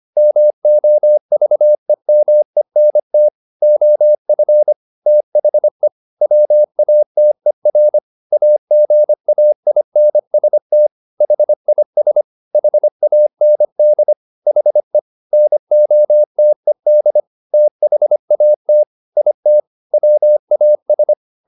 Morse code, heard as a keyed tone, about 25 wpm, 600 Hz. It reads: MOVEMENT OF THE WATER AGAINST HIS HAND HE NOTED THAT IT WAS